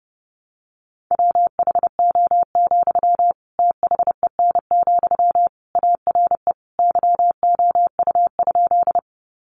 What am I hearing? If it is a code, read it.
Morse code, heard as a keyed tone, about 30 wpm, 700 Hz. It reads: WHO, THEN, ARE YOU?
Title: Sentences from Top 100 Words 30wpm